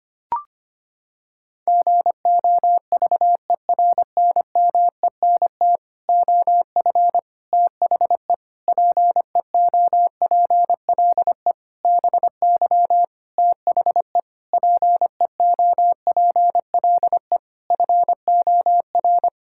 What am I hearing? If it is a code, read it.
Morse code, heard as a keyed tone, about 25 wpm, 700 Hz. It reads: GOVERNMENT OF THE PEOPLE BY THE PEOPLE FOR
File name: practice / 25wpm / Sentences with 9-Word Phrases - 25wpm